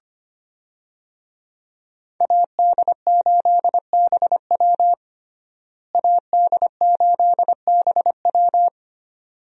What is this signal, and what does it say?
Morse code, heard as a keyed tone, about 25 wpm, 700 Hz. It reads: AD8BW AD8BW
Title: USA Call Signs - 5 Characters - Repeated 2x - 25wpm